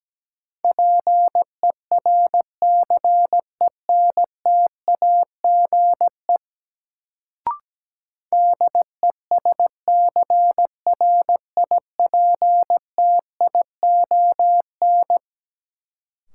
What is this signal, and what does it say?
Morse code, heard as a keyed tone, about 17 wpm, 700 Hz. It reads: PERCENTAGE DESCRIPTION
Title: Sentences from Top 1300 Words - New Word Drill Alt - 17wpm